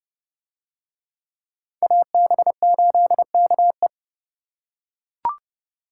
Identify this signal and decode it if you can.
Morse code, heard as a keyed tone, about 28 wpm, 700 Hz. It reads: AB8KE